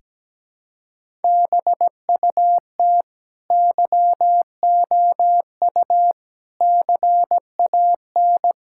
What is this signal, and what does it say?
Morse code, heard as a keyed tone, about 17 wpm, 700 Hz. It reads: BUT YOU CAN
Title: Sentences from Top 1000 Words - Review Part 6 - 17wpm